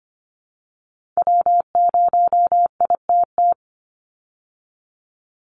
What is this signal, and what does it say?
Morse code, heard as a keyed tone, about 25 wpm, 700 Hz. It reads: W0ITT
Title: Call Signs Repeated 3x - 25wpm